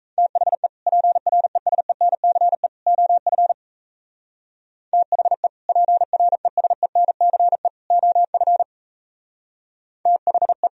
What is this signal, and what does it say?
Morse code, heard as a keyed tone, about 32 wpm, 700 Hz. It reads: THE PRESENCE OF THE PRESENCE OF THE
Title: Speed-Racing - Sets of 3 Words 28wpm